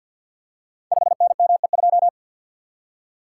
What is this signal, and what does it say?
Morse code, heard as a keyed tone, about 49 wpm, 700 Hz. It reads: 5NME2
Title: Code Groups - 5 Letter:Numbers - 50wpm